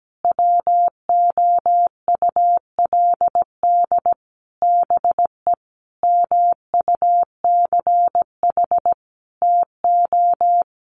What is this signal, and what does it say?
Morse code, heard as a keyed tone, about 17 wpm, 700 Hz. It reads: WOULD BE MUCH TO
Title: Sentences from Top 1200 Words 17wpm